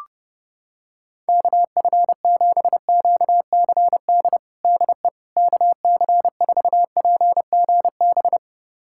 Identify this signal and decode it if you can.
Morse code, heard as a keyed tone, about 32 wpm, 700 Hz. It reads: KF7QCD DE KC4PGB
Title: USA Callsign DE Callsign 30wpm